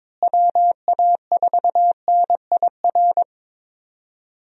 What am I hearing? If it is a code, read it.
Morse code, heard as a keyed tone, about 22 wpm, 700 Hz. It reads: WA4NIR
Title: USA Call Signs - 6 Characters - Repeated 3x - 22wpm